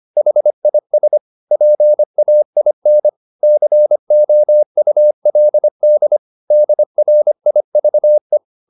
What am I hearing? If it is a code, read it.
Morse code, heard as a keyed tone, about 25 wpm, 600 Hz. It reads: HIS PAIN COULD DRIVE